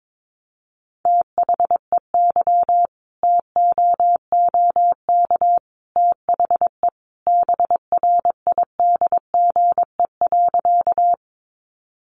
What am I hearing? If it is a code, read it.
Morse code, heard as a keyed tone, about 22 wpm, 700 Hz. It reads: THEY TOOK THE BRIDGE.
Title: Sentences from Top 1200 Words 22wpm